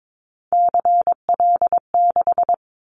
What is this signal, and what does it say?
Morse code, heard as a keyed tone, about 22 wpm, 700 Hz. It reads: CL6
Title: Callsign Prefixes for Top 100 DXCC - 22wpm